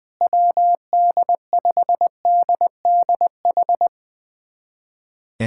Morse code, heard as a keyed tone, about 20 wpm, 700 Hz. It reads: WD5DDH
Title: USA Callsign DE Callsign 20wpm